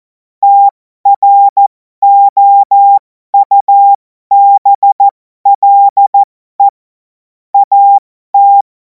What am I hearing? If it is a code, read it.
Morse code, heard as a keyed tone, about 14 wpm, spaced as for 12 wpm, 800 Hz. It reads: TROUBLE AT
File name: Morsekurs / Witze